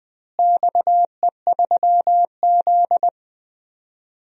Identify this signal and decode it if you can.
Morse code, heard as a keyed tone, about 20 wpm, 700 Hz. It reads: XE3Z
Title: DXCC-100 Call Signs - 4 Characters - Repeated 2x - 20wpm